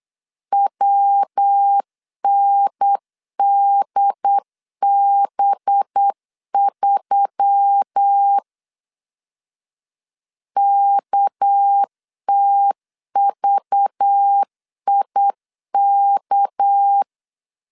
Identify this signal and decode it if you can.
Morse code, heard as a keyed tone, about 8 wpm, 800 Hz. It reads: WNDB3 KTVIK
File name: LEZIONI / 6 PKX0 / 6 PK 8wpm 3003